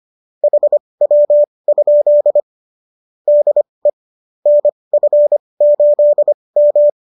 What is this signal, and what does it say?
Morse code, heard as a keyed tone, about 25 wpm, spaced as for 20 wpm, 590 Hz. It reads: HW? DE NF8M
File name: qso221 20